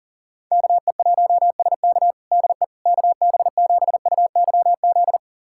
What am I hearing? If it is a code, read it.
Morse code, heard as a keyed tone, about 40 wpm, 700 Hz. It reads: KE1SK DE KB7UYZ
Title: USA Callsign DE Callsign 40wpm